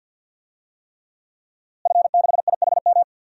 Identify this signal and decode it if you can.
Morse code, heard as a keyed tone, about 53 wpm, 700 Hz. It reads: F6IHM
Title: DXCC-100 Call Signs - 5 Characters - Repeated 3x - 50wpm